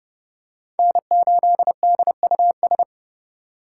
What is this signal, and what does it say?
Morse code, heard as a keyed tone, about 30 wpm, 700 Hz. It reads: N8DUS
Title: K1USN SST QSO Practice - 30wpm